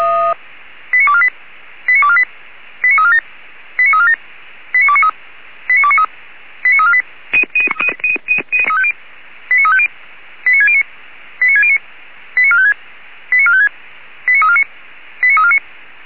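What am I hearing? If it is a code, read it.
Five-tone selective call (ZVEI), heard as quick runs of five tones, tones of about 70 ms. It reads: 89248 89248 89347 89347 89293 89293 89348 89350 89609 89609 89456 89456 89349 89349
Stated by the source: Five-tone ZVEI selective call received on an SDR.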